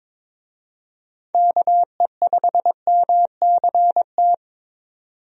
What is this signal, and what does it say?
Morse code, heard as a keyed tone, about 22 wpm, 700 Hz. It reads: KE5MCT
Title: USA Call Signs - 6 Characters - 22wpm